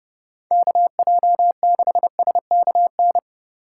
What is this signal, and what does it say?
Morse code, heard as a keyed tone, about 32 wpm, 700 Hz. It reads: KJ6SKN